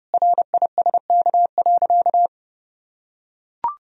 Morse code, heard as a keyed tone, about 32 wpm, 700 Hz. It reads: RISK.
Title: Sentences from Top 1000 Words - Review Part 4 - 30wpm